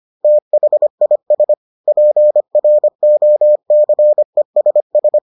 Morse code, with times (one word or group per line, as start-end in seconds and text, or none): 0.2-1.5 THIS
1.9-5.2 PROCESS